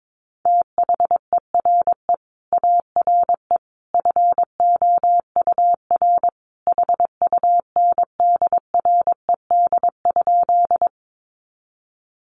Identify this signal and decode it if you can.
Morse code, heard as a keyed tone, about 22 wpm, 700 Hz. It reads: THERE ARE FOUR HUNDRED?
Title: Sentences from Top 400 Words 22wpm